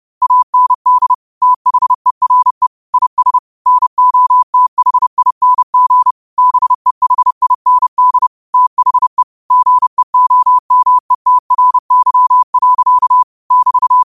Morse code, heard as a keyed tone, about 30 wpm, 1000 Hz. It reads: AND THERE IS NOTHING BEHIND THE GEOMETRY. <BT>
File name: 30wpm